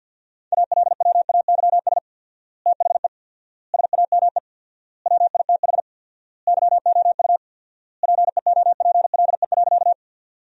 Morse code, heard as a keyed tone, about 50 wpm, 700 Hz. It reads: ALWAYS THE SAME WITH YOU PEOPLE.
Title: 4x Word-Spacing - Sentences from Top 300 Words - 50wpm